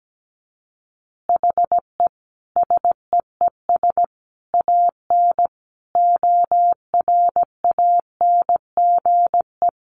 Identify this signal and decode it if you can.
Morse code, heard as a keyed tone, about 17 wpm, 700 Hz. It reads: HE SEES AN ORANGE
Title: Easy Phrases 17wpm